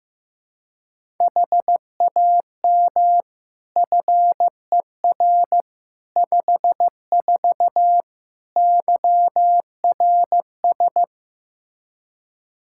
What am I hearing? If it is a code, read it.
Morse code, heard as a keyed tone, about 15 wpm, 700 Hz. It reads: HAM FER 54 YRS
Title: K1USN SST QSO Practice 15wpm